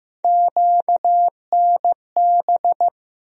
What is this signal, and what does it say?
Morse code, heard as a keyed tone, about 15 wpm, 700 Hz. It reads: QNB
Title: USA Call Sign Postfixes - 3 Letters 15wpm